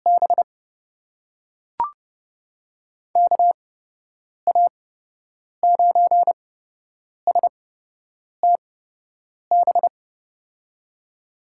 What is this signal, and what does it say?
Morse code, heard as a keyed tone, about 30 wpm, spaced as for 8 wpm, 700 Hz. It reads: B KA9STB
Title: USA Call Signs - 6 Characters - 8x Character Spacing - 30wpm